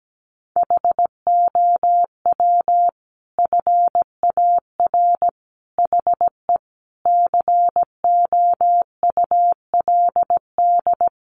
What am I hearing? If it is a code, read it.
Morse code, heard as a keyed tone, about 17 wpm, 700 Hz. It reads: HOW FAR HE COULD
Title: Sentences from Top 1500 Words 17wpm